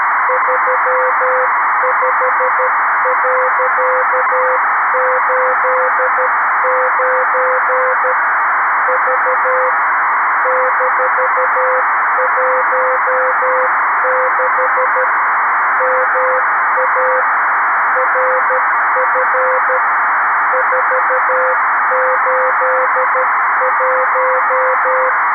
Morse code, heard as a keyed tone, about 13 wpm, 500 Hz, with noise. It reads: 35.89 V -16 MA RF 481
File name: arissat / recordings / ARISSat 11Sept2011 1354utc 60frames